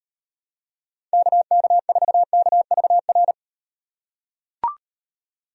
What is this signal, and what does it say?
Morse code, heard as a keyed tone, about 38 wpm, 700 Hz. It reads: KK4KVR